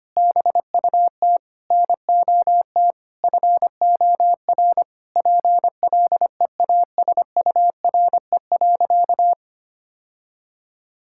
Morse code, heard as a keyed tone, about 25 wpm, 700 Hz. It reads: BUT NOT FOR PLEASURE.
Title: Sentences from Top 1600 Words 25wpm